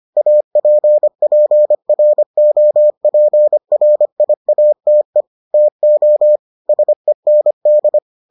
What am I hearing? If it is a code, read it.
Morse code, heard as a keyed tone, about 25 wpm, 600 Hz. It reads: APPROPRIATE TO SEND